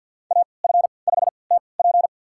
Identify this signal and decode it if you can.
Morse code, heard as a keyed tone, about 51 wpm, spaced as for 28 wpm, 700 Hz. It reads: AF5TP
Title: USA Call Signs - 5 Characters - 3x Character Spacing - 50wpm